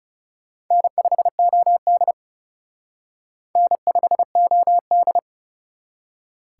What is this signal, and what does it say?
Morse code, heard as a keyed tone, about 37 wpm, 700 Hz. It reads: N5OD N5OD
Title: Speed Racing - USA Call Signs 30wpm